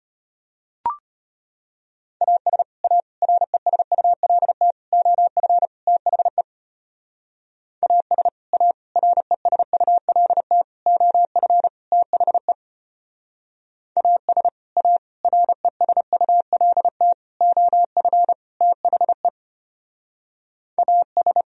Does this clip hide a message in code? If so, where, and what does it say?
Yes: Morse code from 2.2 to 21.4 s, heard as a keyed tone, about 32 wpm, 700 Hz. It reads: AS A RESULT OF THE AS A RESULT OF THE AS A RESULT OF THE AS